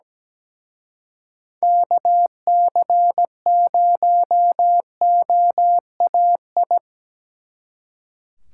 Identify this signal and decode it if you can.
Morse code, heard as a keyed tone, about 20 wpm, 700 Hz. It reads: KC0OAI
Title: Speed Racing - USA 2x3 Call Signs 17wpm